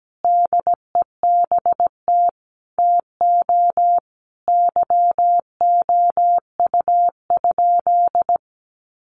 Morse code, heard as a keyed tone, about 17 wpm, 700 Hz. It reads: DEBT TO YOU?
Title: Sentences from Top 1300 Words 17wpm